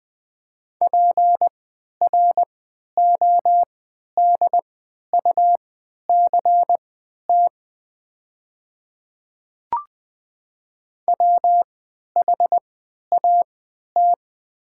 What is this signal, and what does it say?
Morse code, heard as a keyed tone, about 20 wpm, spaced as for 11 wpm, 700 Hz. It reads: PRODUCT WHAT
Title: 3x Character Spacing - Top 500 Words - 20wpm